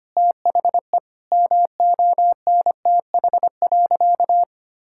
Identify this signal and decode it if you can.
Morse code, heard as a keyed tone, about 25 wpm, 700 Hz. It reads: THE MONTH.
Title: Sentences from Top 1000 Words - Review Part 5 - 25wpm